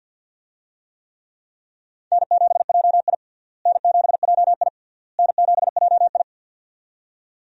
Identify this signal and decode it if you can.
Morse code, heard as a keyed tone, about 46 wpm, 700 Hz. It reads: N7JI N7JI N7JI
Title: USA Call Signs - 4 Characters - Repeated 3x - 50wpm